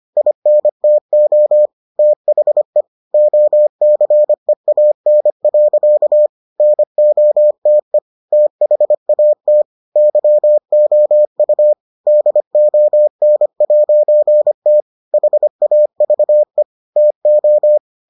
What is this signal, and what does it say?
Morse code, heard as a keyed tone, about 25 wpm, 600 Hz. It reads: INTO THE OCEAN. NOTE THAT YOU DON'T HAVE TO